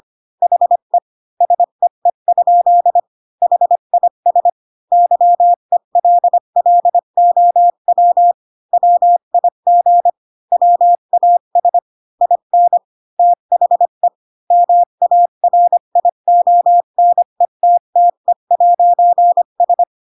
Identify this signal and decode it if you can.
Morse code, heard as a keyed tone, about 25 wpm, 700 Hz. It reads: HE SEE? HIS YELLOW WIG WAS IN THE MARIONETTE'S
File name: Pinocchio / Pinocchio ch3 20 25 ik1hge